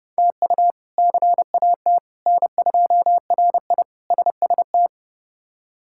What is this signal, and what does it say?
Morse code, heard as a keyed tone, about 30 wpm, 700 Hz. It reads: TU CAT N2RI SST
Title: K1USN SST QSO Practice - 30wpm